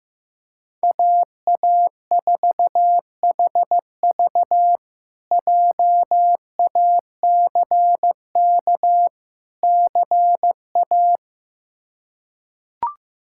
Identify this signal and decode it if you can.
Morse code, heard as a keyed tone, about 16 wpm, 700 Hz. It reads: AA4HV JACK CA
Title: K1USN SST QSO Practice - 15wpm